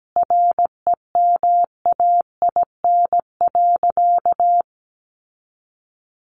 Morse code, heard as a keyed tone, about 17 wpm, 700 Hz. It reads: REMAIN.
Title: Sentences from Top 1200 Words 17wpm